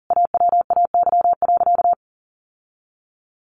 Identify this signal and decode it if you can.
Morse code, heard as a keyed tone, about 40 wpm, 700 Hz. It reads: AWAY.
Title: Sentences from Top 1200 Words 40wpm